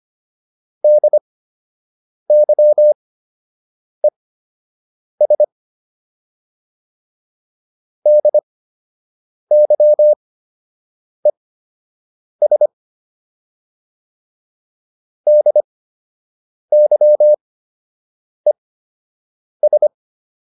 Morse code, heard as a keyed tone, about 25 wpm, spaced as for 7 wpm, 600 Hz. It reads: DYES DYES DYES